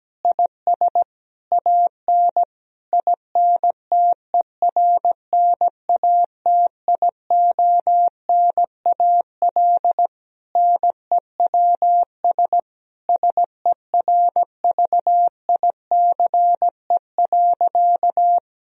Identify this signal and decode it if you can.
Morse code, heard as a keyed tone, about 17 wpm, 700 Hz. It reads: IS AN INTERNATIONAL NEWS SERVICE.
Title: Sentences from Top 1000 Words - Review Part 4 - 17wpm